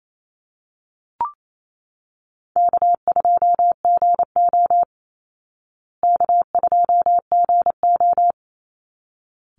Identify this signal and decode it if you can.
Morse code, heard as a keyed tone, about 28 wpm, 690 Hz. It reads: K2GO K2GO
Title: Call Signs Repeated 3x - 28wpm